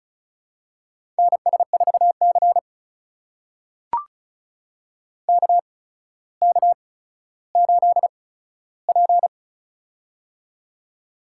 Morse code, heard as a keyed tone, about 36 wpm, spaced as for 15 wpm, 700 Hz. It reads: NS4C KK8P
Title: USA Call Signs - 4 Characters - 8x Character Spacing - 35wpm